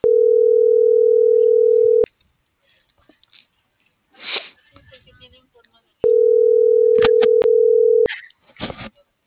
Ringback tone (North America), 440 + 480 Hz.